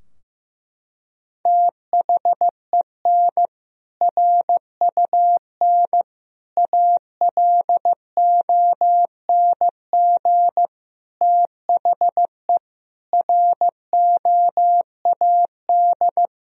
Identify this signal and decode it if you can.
Morse code, heard as a keyed tone, about 15 wpm, 700 Hz. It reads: THEN RUN ALONG THE ROAD
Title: Sentences from Top 1000 Words - Review Part 2 - 15wpm